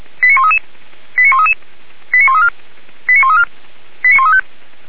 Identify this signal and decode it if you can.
Five-tone selective call (ZVEI), heard as quick runs of five tones, tones of about 70 ms. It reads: 89133 89133 89134 89134 89135
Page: /464/SDR/ZVEI/